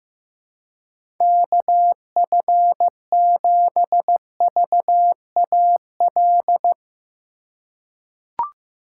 Morse code, heard as a keyed tone, about 15 wpm, 700 Hz. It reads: KF7VAL
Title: USA Call Signs - 6 Characters - 2x Character Spacing - 15wpm